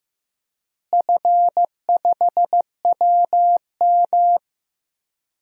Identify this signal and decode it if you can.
Morse code, heard as a keyed tone, about 15 wpm, 700 Hz. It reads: F5WM
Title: DXCC-100 Call Signs - 4 Characters - Repeated 3x - 15wpm